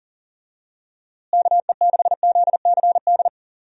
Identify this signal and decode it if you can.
Morse code, heard as a keyed tone, about 37 wpm, 700 Hz. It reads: KE6ZCD
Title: USA Call Signs - 6 Characters - Repeated 2x - 40wpm